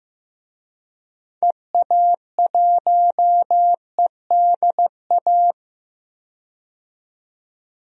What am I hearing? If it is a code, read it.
Morse code, heard as a keyed tone, about 15 wpm, 700 Hz. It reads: EA1EDA
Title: DXCC-100 Call Signs - 6 Characters - 15wpm